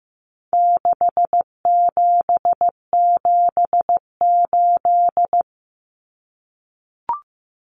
Morse code, heard as a keyed tone, about 15 wpm, 700 Hz. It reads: 6778
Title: Numbers - 4-Digit 15wpm